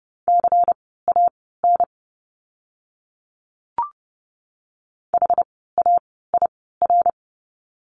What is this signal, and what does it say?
Morse code, heard as a keyed tone, about 30 wpm, spaced as for 17 wpm, 700 Hz. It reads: CAN HAIR
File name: Lesson 17f - Letter u - Words 30wpm